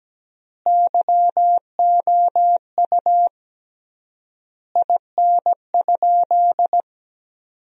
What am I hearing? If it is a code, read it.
Morse code, heard as a keyed tone, about 17 wpm, 700 Hz. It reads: YOU IN?
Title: 3x Word-Spacing - Sentences from Top 300 Words - 17wpm